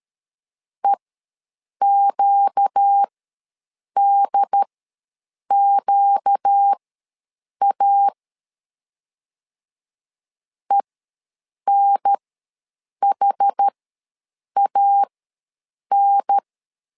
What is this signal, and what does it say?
Morse code, heard as a keyed tone, about 13 wpm, spaced as for 7 wpm, 800 Hz. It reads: EQDQA ENHAN
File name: LEZIONI / 4 Q ws 12wpm 1912